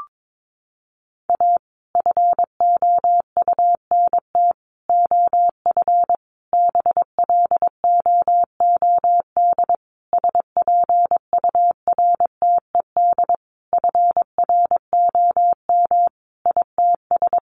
Morse code, heard as a keyed tone, about 22 wpm, 700 Hz. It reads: A FOUNT OF BLOOD SPURTED FROM ITS
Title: A Sound Of Thunder 22wpm